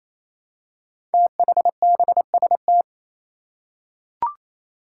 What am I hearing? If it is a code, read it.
Morse code, heard as a keyed tone, about 28 wpm, 700 Hz. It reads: THBST
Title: Code Groups - 5 Letters - 28wpm